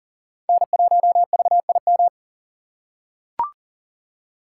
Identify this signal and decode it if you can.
Morse code, heard as a keyed tone, about 37 wpm, 700 Hz. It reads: N1VIM
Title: USA Call Signs - 5 Characters - 2x Character Spacing - 40wpm